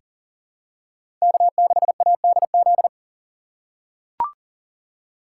Morse code, heard as a keyed tone, about 40 wpm, 700 Hz. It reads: K6ADZ
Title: USA Call Signs - 3x Character Spacing - 40wpm